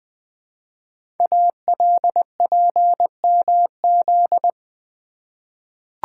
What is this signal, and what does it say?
Morse code, heard as a keyed tone, about 20 wpm, 700 Hz. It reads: ALPMZ
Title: Code Groups - 5 Letter:Numbers - 20wpm